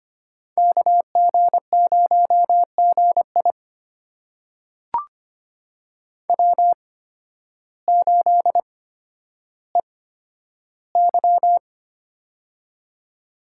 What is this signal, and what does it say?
Morse code, heard as a keyed tone, about 25 wpm, spaced as for 12 wpm, 700 Hz. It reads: KG0GI W8EY
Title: USA Call Signs - 8x Character Spacing - 25wpm